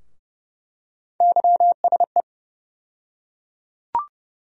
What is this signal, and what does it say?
Morse code, heard as a keyed tone, about 30 wpm, 700 Hz. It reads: YSE
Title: USA Call Sign Postfixes - 3 Letters 30wpm